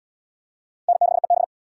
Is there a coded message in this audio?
Yes: Morse code from 0.9 to 1.4 s, heard as a keyed tone, about 77 wpm, 700 Hz. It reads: N6EB